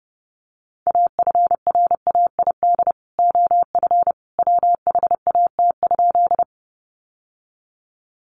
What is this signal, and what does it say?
Morse code, heard as a keyed tone, about 30 wpm, 700 Hz. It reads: AFRAID OF WHAT?
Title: Sentences from Top 1500 Words 30wpm